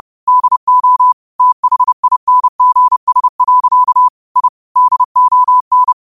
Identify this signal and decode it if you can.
Morse code, heard as a keyed tone, about 30 wpm, 1000 Hz. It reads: DO THINGS. I DON
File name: qotdfiles / 30wpm